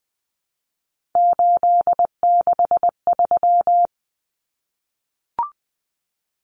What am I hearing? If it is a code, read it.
Morse code, heard as a keyed tone, about 20 wpm, 700 Hz. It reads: 863
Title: Common Cut Numbers - 3-Digit 20wpm